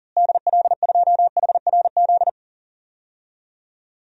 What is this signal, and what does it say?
Morse code, heard as a keyed tone, about 40 wpm, 700 Hz. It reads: DL2HRZ
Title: DXCC-100 Call Signs - 6 Characters - Repeated 3x - 40wpm